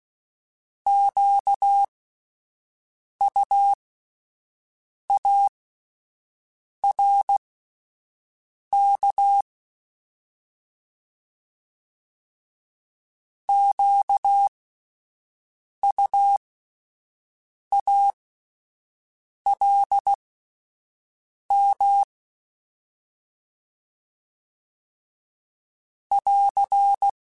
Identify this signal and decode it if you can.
Morse code, heard as a keyed tone, about 16 wpm, spaced as for 5 wpm, 800 Hz. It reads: QUARK QUALM <AR>
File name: Alphabetical / Alphabetical mp3s / Qs @5